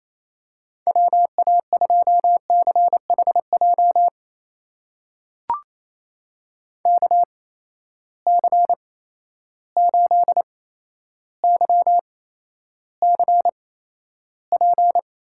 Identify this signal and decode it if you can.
Morse code, heard as a keyed tone, about 28 wpm, spaced as for 12 wpm, 700 Hz. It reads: WA2CHJ KC8YCP
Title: USA Call Signs - 6 Characters - 8x Character Spacing - 28wpm